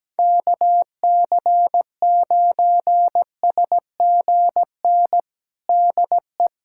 Morse code, heard as a keyed tone, about 17 wpm, 700 Hz. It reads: KC9SGN DE
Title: USA Callsign DE Callsign 17wpm